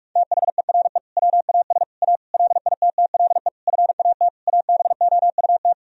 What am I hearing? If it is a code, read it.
Morse code, heard as a keyed tone, about 46 wpm, 700 Hz. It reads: THERE WAS A LITTLE FAT ABOUT
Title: Sentences from Top 1000 Words - Review Part 4 - 45wpm